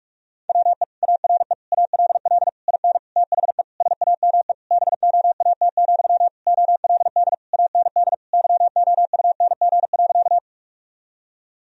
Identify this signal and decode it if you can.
Morse code, heard as a keyed tone, about 46 wpm, 700 Hz. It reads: WE ARE ALL IN THE SAME BOAT, OLD AND YOUNG.
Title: Sentences from Top 1000 Words - Review Part 3 - 45wpm